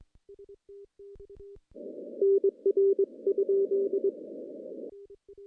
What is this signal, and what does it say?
Morse code, heard as a keyed tone, about 22 wpm, 390 Hz. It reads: NR?
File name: contests / CW-SS-k5tr- -1500